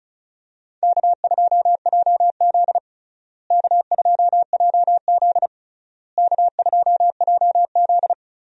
Morse code, heard as a keyed tone, about 34 wpm, 700 Hz. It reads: K2JZ K2JZ K2JZ
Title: USA Call Signs - 4 Characters - Repeated 3x - 35wpm